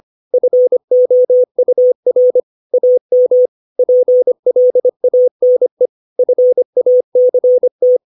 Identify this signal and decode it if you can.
Morse code, heard as a keyed tone, about 25 wpm, 500 Hz. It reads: FOUR AM PLANE FACT
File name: 632 words / 632 Words 25WPM